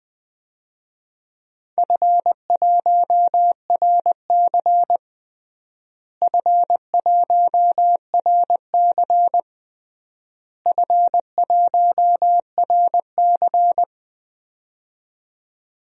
Morse code, heard as a keyed tone, about 20 wpm, 700 Hz. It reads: F1RC F1RC F1RC
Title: DXCC-100 Call Signs - 4 Characters - Repeated 3x - 20wpm